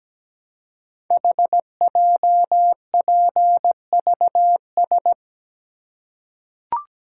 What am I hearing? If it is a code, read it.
Morse code, heard as a keyed tone, about 17 wpm, 700 Hz. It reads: HJPVS
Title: Code Groups - 5 Letters - 17wpm